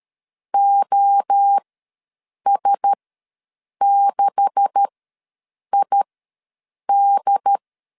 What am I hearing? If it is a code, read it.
Morse code, heard as a keyed tone, about 13 wpm, spaced as for 7 wpm, 800 Hz. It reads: OS6ID
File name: LEZIONI / 9 286 ws 12wpm 3351